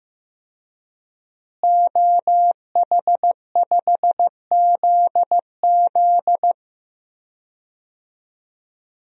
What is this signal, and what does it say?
Morse code, heard as a keyed tone, about 15 wpm, 700 Hz. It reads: OH5ZZ
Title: DXCC-100 Call Signs - 5 Characters - Repeated 3x - 15wpm